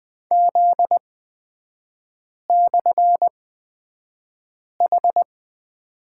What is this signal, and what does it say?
Morse code, heard as a keyed tone, about 20 wpm, spaced as for 5 wpm, 700 Hz. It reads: Z/H
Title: 243 Lesson 39e - Letter z - Three Characters 20wpm